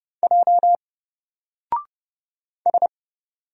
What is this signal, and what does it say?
Morse code, heard as a keyed tone, about 30 wpm, 700 Hz. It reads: J S
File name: Single-Letter-Number-30wpm